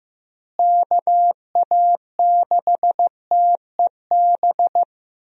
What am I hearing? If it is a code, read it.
Morse code, heard as a keyed tone, about 16 wpm, 700 Hz. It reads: KA6TEB
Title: USA Call Signs - 6 Characters - Repeated 3x - 15wpm